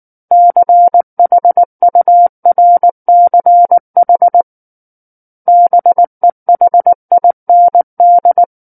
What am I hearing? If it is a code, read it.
Morse code, heard as a keyed tone, about 19 wpm, 700 Hz. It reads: CHURCH BEHIND
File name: Book One / 10-Chapter 10 - 19 WPM